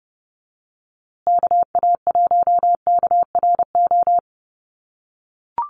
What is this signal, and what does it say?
Morse code, heard as a keyed tone, about 30 wpm, 700 Hz. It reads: KA1KRO
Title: USA Call Signs Repeated 2x 30wpm